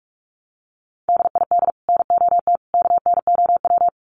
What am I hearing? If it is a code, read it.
Morse code, heard as a keyed tone, about 44 wpm, 700 Hz. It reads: DID NOT KNOW
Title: Sets of 3 Words 45wpm